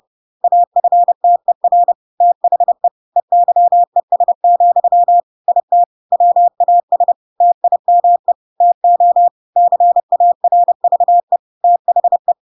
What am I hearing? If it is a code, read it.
Morse code, heard as a keyed tone, about 30 wpm, 700 Hz. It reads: AFTER THE EYES, IT WAS TIME TO CARVE THE
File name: Pinocchio / Pinocchio ch3 30 30 ik1hge